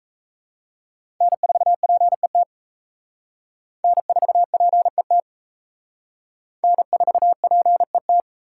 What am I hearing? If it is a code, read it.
Morse code, heard as a keyed tone, about 34 wpm, 700 Hz. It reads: N4PET N4PET N4PET
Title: Speed Racing - USA 1x3 Call Signs 28wpm